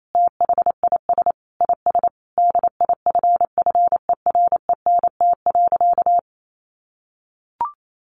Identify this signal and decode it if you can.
Morse code, heard as a keyed tone, about 28 wpm, 700 Hz. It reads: THIS IS DIFFERENT.
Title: Sentences from Top 600 Words 28wpm